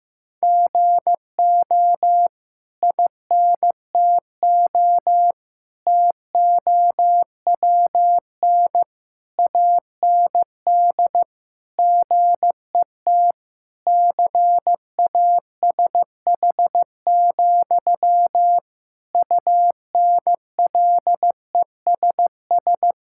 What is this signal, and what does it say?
Morse code, heard as a keyed tone, about 16 wpm, 700 Hz. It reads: GO INTO TOWN AND GET CASH, UNLESS
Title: Sentences from Top 1000 Words - Review Part 6 - 15wpm